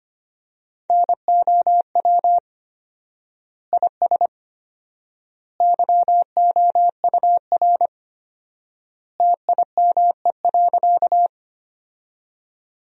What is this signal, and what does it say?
Morse code, heard as a keyed tone, about 25 wpm, 700 Hz. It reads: NOW IS YOUR TIME.
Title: 4x Word-Spacing - Sentences from Top 100 Words - 25wpm